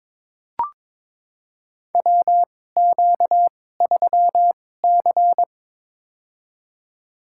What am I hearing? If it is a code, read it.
Morse code, heard as a keyed tone, about 22 wpm, spaced as for 16 wpm, 700 Hz. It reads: WQ3C
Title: USA Call Signs - 4 Characters - 2x Character Spacing - 22wpm